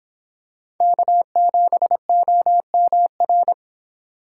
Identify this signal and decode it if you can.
Morse code, heard as a keyed tone, about 24 wpm, 700 Hz. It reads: K7OMR